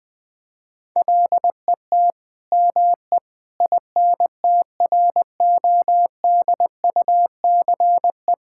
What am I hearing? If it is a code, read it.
Morse code, heard as a keyed tone, about 21 wpm, 700 Hz. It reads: LET ME INTRODUCE
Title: Sentences from Top 1600 Words 20wpm